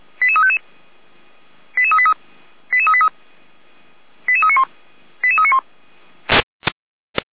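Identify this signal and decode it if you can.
Five-tone selective call (ZVEI), heard as quick runs of five tones, tones of about 70 ms. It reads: 88355 88383 88383 88391 88391
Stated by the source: ZVEI five-tone call 88335 received by SDR, logged by feuerwehr-orlamuende.org fire brigade server.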